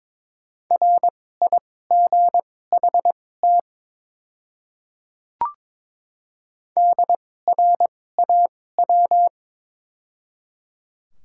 Morse code, heard as a keyed tone, about 22 wpm, spaced as for 16 wpm, 700 Hz. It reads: RIGHT DRAW
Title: 2x Character Spacing - Top 500 Words - 22wpm